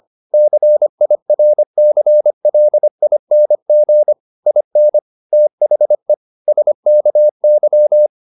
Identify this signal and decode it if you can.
Morse code, heard as a keyed tone, about 25 wpm, 600 Hz. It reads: CIRCLING IN THE SKY
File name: the old man and the sea part 1 25wpm